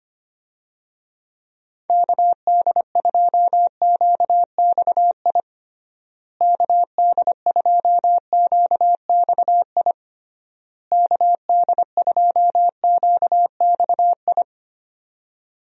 Morse code, heard as a keyed tone, about 25 wpm, 700 Hz. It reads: KD2QXI KD2QXI KD2QXI
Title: USA Call Signs - 6 Characters - Repeated 3x - 25wpm